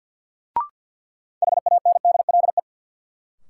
Morse code, heard as a keyed tone, about 52 wpm, 700 Hz. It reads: HANDLE
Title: Sentences from Top 1100 Words - New Word Drill - 50wpm